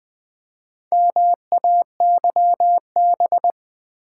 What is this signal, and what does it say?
Morse code, heard as a keyed tone, about 20 wpm, 700 Hz. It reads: MAYB
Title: Build A Word - 1 Letter At A Time - 5-Letter Words - 20wpm